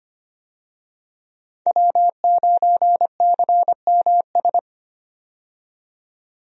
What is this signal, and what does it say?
Morse code, heard as a keyed tone, about 25 wpm, 700 Hz. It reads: W9CMS
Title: USA Call Signs - 5 Characters - 25wpm